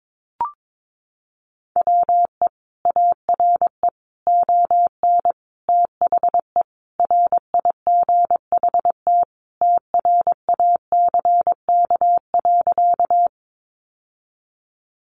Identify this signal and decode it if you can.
Morse code, heard as a keyed tone, about 22 wpm, 700 Hz. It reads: WE ARE ON THE RIGHT TRACK.
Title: Sentences from Top 1100 Words 22wpm